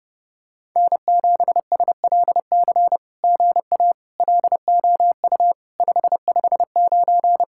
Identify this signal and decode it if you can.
Morse code, heard as a keyed tone, about 30 wpm, 700 Hz. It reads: N7SLC GA LOU 559
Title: SOTA QSO Practice 30wpm